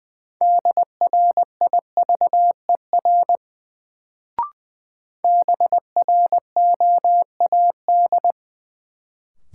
Morse code, heard as a keyed tone, about 20 wpm, 700 Hz. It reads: DRIVER BROAD
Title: Sentences from Top 1000 Words - New Word Drill - 20wpm